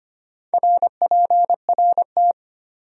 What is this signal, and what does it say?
Morse code, heard as a keyed tone, about 31 wpm, 700 Hz. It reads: RPRT
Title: Speed-Racing Abbreviations - 25wpm